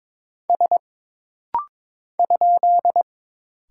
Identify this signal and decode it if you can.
Morse code, heard as a keyed tone, about 22 wpm, 700 Hz. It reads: S ?
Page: This rapid-fire timing is geared to push your ICR (Instant Character Recognition) skills.